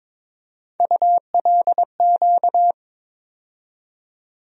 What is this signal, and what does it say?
Morse code, heard as a keyed tone, about 21 wpm, 700 Hz. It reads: ULQ
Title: USA Call Sign Postfixes - 3 Letters 22wpm